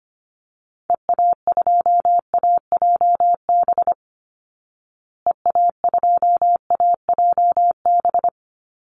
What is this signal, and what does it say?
Morse code, heard as a keyed tone, about 25 wpm, 700 Hz. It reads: EA2AJB EA2AJB
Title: Call Signs Repeated 2x - 25wpm